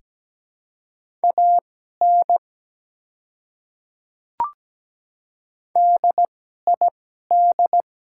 Morse code, heard as a keyed tone, about 17 wpm, spaced as for 12 wpm, 700 Hz. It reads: AN DID
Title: 2x Character Spacing - Top 100 Words - 17wpm